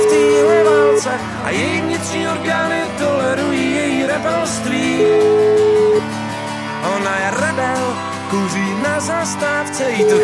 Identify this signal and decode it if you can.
Ringback tone (Europe, 425 Hz), with speech.